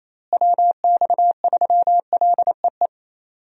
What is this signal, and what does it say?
Morse code, heard as a keyed tone, about 28 wpm, 700 Hz. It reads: WX3LEE
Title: USA Call Signs - 6 Characters - Repeated 2x - 28wpm